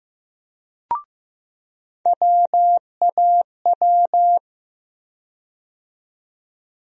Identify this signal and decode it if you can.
Morse code, heard as a keyed tone, about 15 wpm, 700 Hz. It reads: WAW
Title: USA Call Sign Postfixes - 3 Letters 15wpm